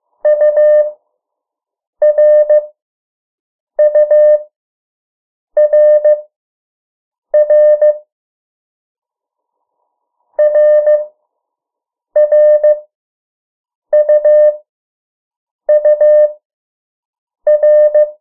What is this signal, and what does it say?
Morse code, heard as a keyed tone, about 15 wpm, spaced as for 6 wpm, 600 Hz. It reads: URURR RRUUR
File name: morse mp3 / 03 urok 2